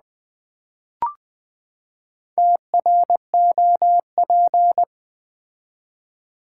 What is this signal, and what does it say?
Morse code, heard as a keyed tone, about 20 wpm, 700 Hz. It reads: TROP
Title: Build A Word - 4 Letters At A Time - 8-Letter Words - 20wpm